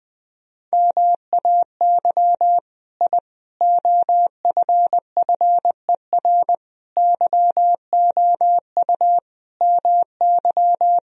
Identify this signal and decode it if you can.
Morse code, heard as a keyed tone, about 20 wpm, 700 Hz. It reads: MAY I OFFER YOU MY
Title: Sentences from Top 1000 Words - Review Part 4 - 20wpm